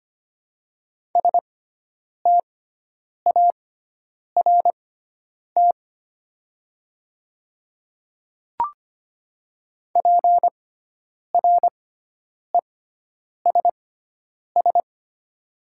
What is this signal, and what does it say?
Morse code, heard as a keyed tone, about 25 wpm, spaced as for 9 wpm, 700 Hz. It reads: START PRESS
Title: 6x Character Spacing - Top 300 Words - 25wpm